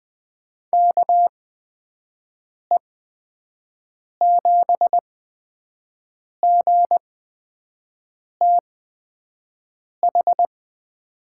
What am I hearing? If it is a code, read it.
Morse code, heard as a keyed tone, about 20 wpm, spaced as for 5 wpm, 700 Hz. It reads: KE7GTH
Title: USA Call Signs - 8x Character Spacing - 20wpm